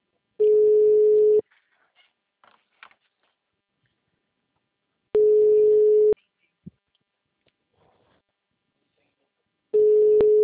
Ringback tone (Europe, 425 Hz).